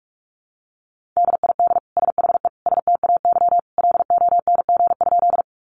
Morse code, heard as a keyed tone, about 44 wpm, 700 Hz. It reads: DID SHE STAY LONG?
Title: Sentences from Top 500 Words 45wpm